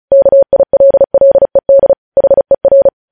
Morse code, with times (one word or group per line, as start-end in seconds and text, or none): 0.1-1.9 KILLED
2.2-2.9 HER